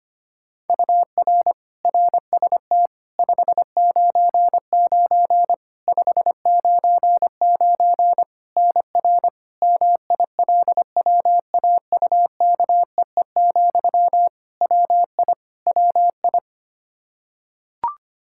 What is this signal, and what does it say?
Morse code, heard as a keyed tone, about 25 wpm, 700 Hz. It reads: UR RST 599 599 NR MILWAUKEE, WI WI
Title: Practice QSO - Exchange 1 - Beginning - 25wpm